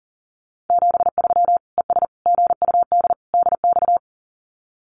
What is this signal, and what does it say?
Morse code, heard as a keyed tone, about 40 wpm, 700 Hz. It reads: 73 ES GUD DX
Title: Common QSO Phrases 40wpm